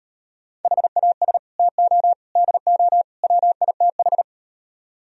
Morse code, heard as a keyed tone, about 38 wpm, 700 Hz. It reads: HAS TO DO WITH